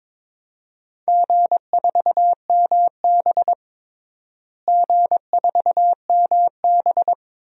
Morse code, heard as a keyed tone, about 21 wpm, 700 Hz. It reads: G4MB G4MB
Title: DXCC-100 Call Signs - 4 Characters - Repeated 2x - 22wpm